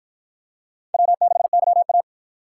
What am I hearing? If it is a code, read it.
Morse code, heard as a keyed tone, about 53 wpm, 700 Hz. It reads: W6YA